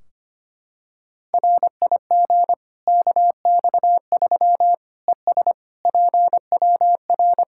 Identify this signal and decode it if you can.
Morse code, heard as a keyed tone, about 25 wpm, 700 Hz. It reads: RIG KX3 ES PWR
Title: K1USN SST QSO Practice 25wpm